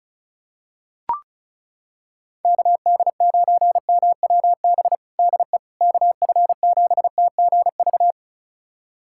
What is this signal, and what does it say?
Morse code, heard as a keyed tone, about 36 wpm, 700 Hz. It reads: KD9MWB DE KF7TGV
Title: USA Callsign DE Callsign - 6 Characters - 35wpm